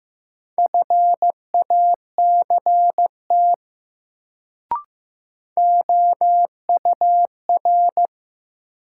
Morse code, heard as a keyed tone, about 15 wpm, 700 Hz. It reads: FACT OUR
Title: Sets of 2 Words - New Word Drill - 15wpm